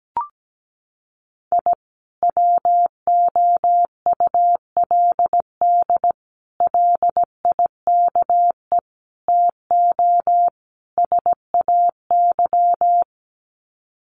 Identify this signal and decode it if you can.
Morse code, heard as a keyed tone, about 17 wpm, 700 Hz. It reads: I WOULD LIKE TO SAY
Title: Sets of 5 Words Encore2 17wpm